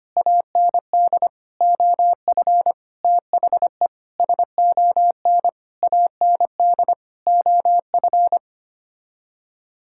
Morse code, heard as a keyed tone, about 25 wpm, 700 Hz. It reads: AND OF THE SON AND OF
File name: Sets of 9 Words - 25wpm